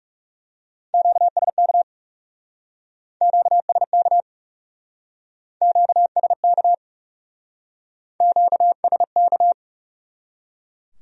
Morse code, heard as a keyed tone, about 39 wpm, 700 Hz. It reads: QSK QSK QSK QSK